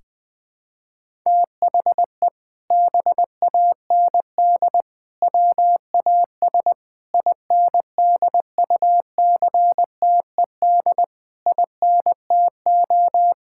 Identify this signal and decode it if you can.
Morse code, heard as a keyed tone, about 20 wpm, 700 Hz. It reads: THE BAND WAS INDUCTED INTO
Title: Sentences with 6-Word Phrases - 20wpm